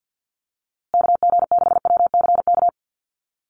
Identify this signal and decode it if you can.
Morse code, heard as a keyed tone, about 50 wpm, 700 Hz. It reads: KG6WCK
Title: USA Call Signs Repeated 3x 50wpm